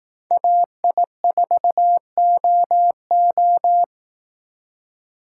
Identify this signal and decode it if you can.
Morse code, heard as a keyed tone, about 18 wpm, 700 Hz. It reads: AI4OO
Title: Speed Racing - USA Call Signs 15wpm